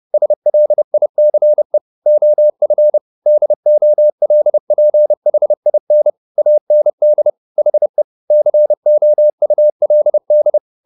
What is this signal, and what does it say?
Morse code, heard as a keyed tone, about 30 wpm, 600 Hz. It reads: SLICE OF DOLPHIN AND HE COULD